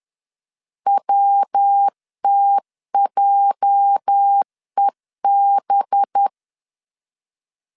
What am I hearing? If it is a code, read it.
Morse code, heard as a keyed tone, about 10 wpm, 800 Hz. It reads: WTJEB
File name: LEZIONI / 8 YJ45 / 8 YJ45 10wpm 2549